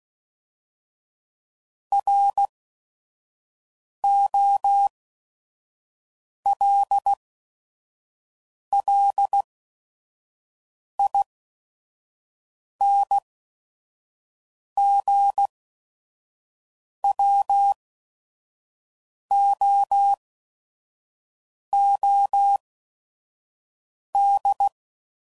Morse code, heard as a keyed tone, about 16 wpm, spaced as for 5 wpm, 800 Hz. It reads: ROLLINGWOOD
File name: CW Code Groups @5WPM / fromSM's / Cities